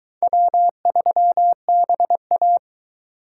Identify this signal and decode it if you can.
Morse code, heard as a keyed tone, about 22 wpm, 700 Hz. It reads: W3BA